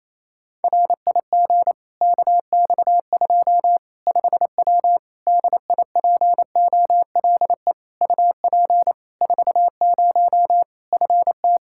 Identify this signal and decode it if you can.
Morse code, heard as a keyed tone, about 28 wpm, 700 Hz. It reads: RIG KX2 5W DIPOLE UP 40 FT
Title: Practice QSO - Exchange 2 - Info - 28wpm